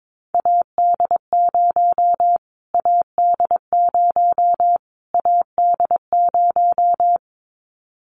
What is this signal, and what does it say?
Morse code, heard as a keyed tone, about 22 wpm, 700 Hz. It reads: AD0 AD0 AD0
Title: Callsign Prefixes for Top 25 DXCC - Repeated 3x - 22wpm